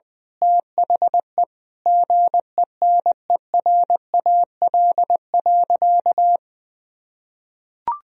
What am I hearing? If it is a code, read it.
Morse code, heard as a keyed tone, about 20 wpm, 700 Hz. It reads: THE GENERAL.
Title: Sentences from Top 1000 Words - Review Part 3 - 20wpm